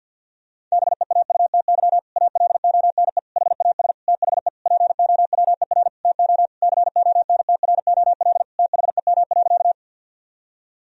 Morse code, heard as a keyed tone, about 51 wpm, 700 Hz. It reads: BEAUTY ALONE HAS THE POWER TO CONTROL THEM.
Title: Sentences from Top 1000 Words - Review Part 2 - 50wpm